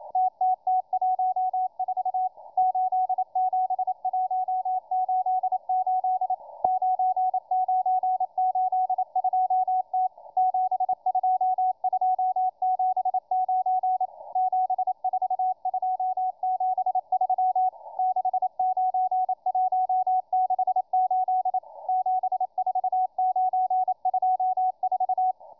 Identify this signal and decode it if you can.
Morse code, heard as a keyed tone, about 28 wpm, 740 Hz, with radio noise. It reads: TTT14 87188 9982T 72279 74273 69168 74924